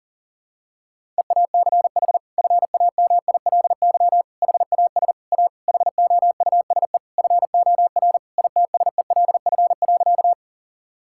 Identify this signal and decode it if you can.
Morse code, heard as a keyed tone, about 37 wpm, 700 Hz. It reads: EACH FAMILY HAS A HOUSE FOR ITSELF.
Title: Sentences from Top 1000 Words - Review Part 5 - 40wpm